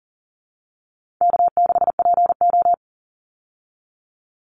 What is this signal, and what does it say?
Morse code, heard as a keyed tone, about 40 wpm, 700 Hz. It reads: K6PO
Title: Call Signs Repeated 3x - 40wpm